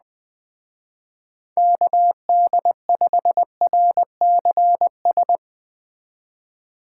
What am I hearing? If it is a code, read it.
Morse code, heard as a keyed tone, about 19 wpm, 700 Hz. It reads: KD5RCS
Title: Speed Racing - USA 2x3 Call Signs 17wpm